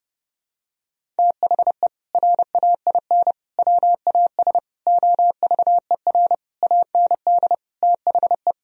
Morse code, heard as a keyed tone, about 32 wpm, 700 Hz. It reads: THE RAIN WAS OVER AND THE
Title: Sentences from Top 1000 Words - Review Part 1 - 30wpm